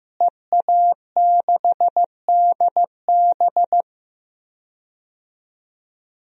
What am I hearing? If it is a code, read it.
Morse code, heard as a keyed tone, about 15 wpm, 700 Hz. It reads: EA6DB
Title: DXCC-100 Call Signs - 5 Characters - Repeated 3x - 15wpm